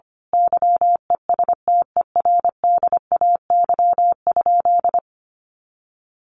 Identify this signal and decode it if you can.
Morse code, heard as a keyed tone, about 25 wpm, 700 Hz. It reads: YESTERDAY?
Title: A Sound Of Thunder 25wpm